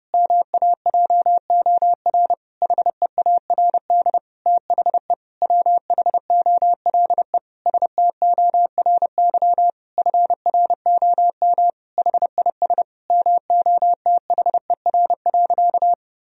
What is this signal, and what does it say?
Morse code, heard as a keyed tone, about 30 wpm, 700 Hz. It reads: MAJOR HEARD THE WHOLE STORY FROM HIS MOTHER.
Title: Sentences from Top 1000 Words - Review Part 4 - 30wpm